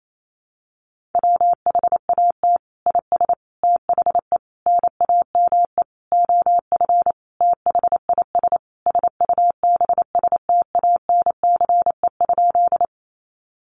Morse code, heard as a keyed tone, about 28 wpm, 700 Hz. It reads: WHAT IS THE NAME OF THIS SUBSTANCE?
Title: Sentences from Top 1500 Words 28wpm